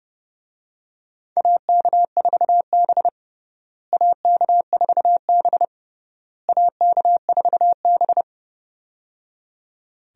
Morse code, heard as a keyed tone, about 30 wpm, 700 Hz. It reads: AK4B AK4B AK4B
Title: USA Call Signs - 4 Characters - Repeated 3x - 30wpm